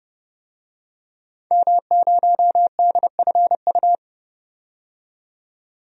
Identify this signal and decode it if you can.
Morse code, heard as a keyed tone, about 30 wpm, 700 Hz. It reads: M0DFU
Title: DXCC-100 Call Signs - 5 Characters - 30wpm